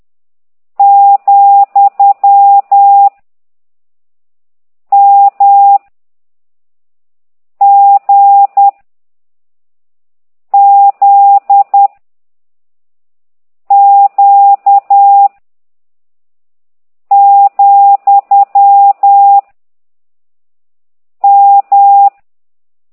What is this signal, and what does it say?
Morse code, heard as a keyed tone, about 10 wpm, 800 Hz. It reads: , M G Z Q , M